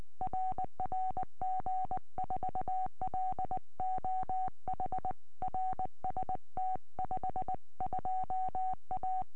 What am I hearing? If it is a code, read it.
Morse code, heard as a keyed tone, about 20 wpm, 750 Hz. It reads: RR G4LOH RST 52A